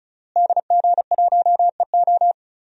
Morse code, heard as a keyed tone, about 36 wpm, 700 Hz. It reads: DG1EO
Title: DXCC-100 Call Signs - 5 Characters - Repeated 3x - 35wpm